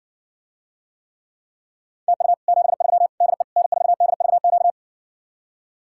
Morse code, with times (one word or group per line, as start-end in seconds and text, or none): 2.1-2.3 TU
2.5-3.1 73
3.2-3.4 DE
3.6-4.7 N4DVQ